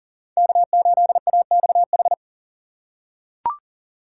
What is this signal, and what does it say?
Morse code, heard as a keyed tone, about 40 wpm, 700 Hz. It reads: K8AXH
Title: USA Call Signs - 5 Characters - 3x Character Spacing - 40wpm